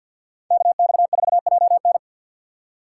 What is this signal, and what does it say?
Morse code, heard as a keyed tone, about 53 wpm, 700 Hz. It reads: KX4JN
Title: USA Call Signs - 5 Characters - 6x Character Spacing - 50wpm